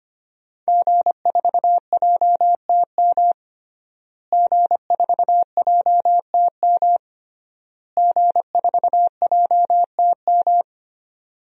Morse code, heard as a keyed tone, about 25 wpm, 700 Hz. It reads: G4JTM G4JTM G4JTM
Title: DXCC-100 Call Signs - 5 Characters - Repeated 3x - 25wpm